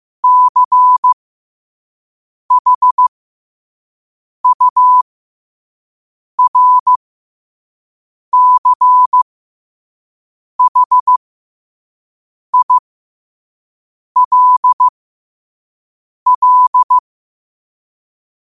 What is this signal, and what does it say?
Morse code, heard as a keyed tone, about 15 wpm, spaced as for 5 wpm, 1000 Hz. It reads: CHURCHILL
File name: qotdfiles / 5wpm